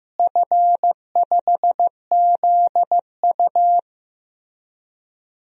Morse code, heard as a keyed tone, about 15 wpm, 700 Hz. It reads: F5ZU
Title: DXCC-100 Call Signs - 4 Characters - Repeated 3x - 15wpm